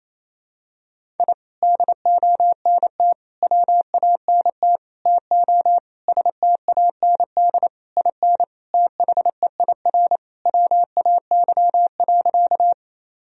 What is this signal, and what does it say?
Morse code, heard as a keyed tone, about 28 wpm, 700 Hz. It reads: I DONT WANT TO STAND IN THEIR WAY.
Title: Sentences from Top 1000 Words - Review Part 1 - 28wpm